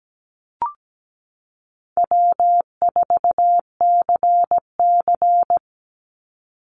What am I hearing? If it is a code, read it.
Morse code, heard as a keyed tone, about 17 wpm, 700 Hz. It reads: W4CC
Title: Easy Call Signs Repeated 3x - 17wpm